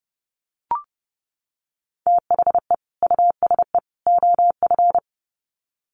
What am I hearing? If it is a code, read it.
Morse code, heard as a keyed tone, about 30 wpm, 700 Hz. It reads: THE USE OF
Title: Sets of 3 Words 30wpm